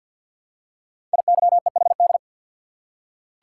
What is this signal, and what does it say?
Morse code, heard as a keyed tone, about 47 wpm, 700 Hz. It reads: IYEHD
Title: Code Groups - 5 Letters - 50wpm